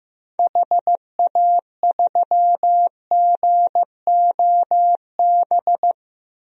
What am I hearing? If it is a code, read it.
Morse code, heard as a keyed tone, about 15 wpm, 700 Hz. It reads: HA3GOB
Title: DXCC-100 Call Signs - 6 Characters - Repeated 3x - 15wpm